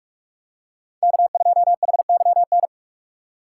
Morse code, heard as a keyed tone, about 44 wpm, 690 Hz. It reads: K2HYN
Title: USA Call Signs - 5 Characters - 2x Character Spacing - 45wpm